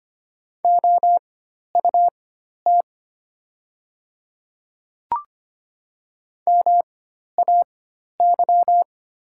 Morse code, heard as a keyed tone, about 25 wpm, spaced as for 12 wpm, 700 Hz. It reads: OUT MAY
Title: 4x Character Spacing - Top 100 Words - 25wpm